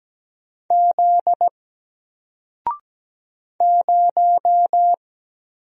Morse code, heard as a keyed tone, about 17 wpm, 700 Hz. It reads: Z 0
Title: Single Letter-Number Alt 17wpm